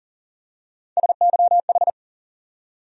Morse code, heard as a keyed tone, about 40 wpm, 700 Hz. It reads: SYH
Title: USA Call Sign Postfixes - 3 Letters 40wpm